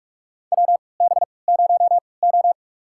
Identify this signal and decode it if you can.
Morse code, heard as a keyed tone, about 43 wpm, spaced as for 25 wpm, 700 Hz. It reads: WB0O
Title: USA Call Signs - 4 Characters - 3x Character Spacing - 45wpm